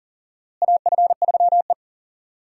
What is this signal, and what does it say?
Morse code, heard as a keyed tone, about 40 wpm, 700 Hz. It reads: AF3E
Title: USA Call Signs - 4 Characters - Repeated 3x - 40wpm